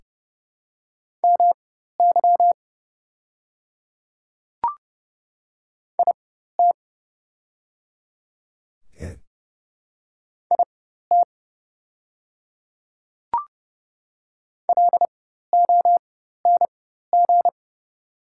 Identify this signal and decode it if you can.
Morse code, heard as a keyed tone, about 29 wpm, spaced as for 14 wpm, 700 Hz. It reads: MY IT IT LONG
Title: 4x Character Spacing - Top 100 Words - 30wpm